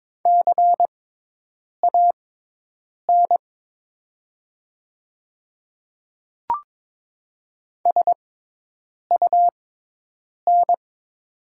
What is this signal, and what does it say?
Morse code, heard as a keyed tone, about 22 wpm, spaced as for 8 wpm, 700 Hz. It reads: CAN SUN
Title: 6x Character Spacing - Top 500 Words - 22wpm